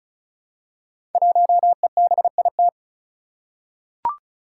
Morse code, heard as a keyed tone, about 36 wpm, 700 Hz. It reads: 1EBIT
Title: Five Challenging Letter-Number Combinations 35wpm